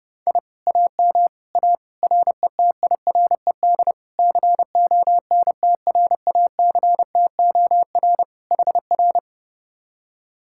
Morse code, heard as a keyed tone, about 31 wpm, 700 Hz. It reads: I AM A RETIRED CONTRACTOR HR